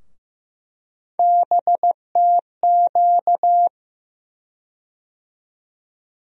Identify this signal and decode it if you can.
Morse code, heard as a keyed tone, about 15 wpm, 700 Hz. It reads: BTQ
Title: USA Call Sign Postfixes - 3 Letters 15wpm